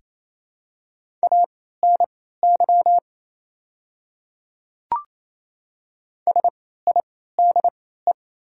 Morse code, heard as a keyed tone, about 28 wpm, spaced as for 16 wpm, 700 Hz. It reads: ANY SIDE
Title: 3x Character Spacing - Top 300 Words - 28wpm